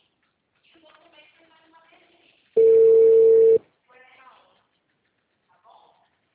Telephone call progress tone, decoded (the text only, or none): ringback tone